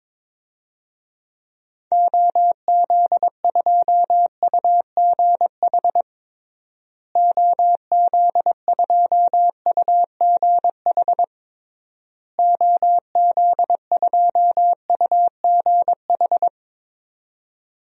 Morse code, heard as a keyed tone, about 22 wpm, 700 Hz. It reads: OZ2UGH OZ2UGH OZ2UGH
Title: DXCC-100 Call Signs - 6 Characters - Repeated 3x - 22wpm